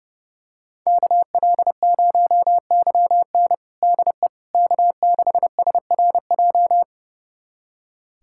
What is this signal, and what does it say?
Morse code, heard as a keyed tone, about 32 wpm, 700 Hz. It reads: KL0YN DE K6SRJ
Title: USA Callsign DE Callsign - 5 Characters - 30wpm